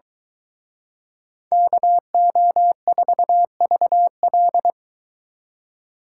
Morse code, heard as a keyed tone, about 23 wpm, 700 Hz. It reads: KO4VL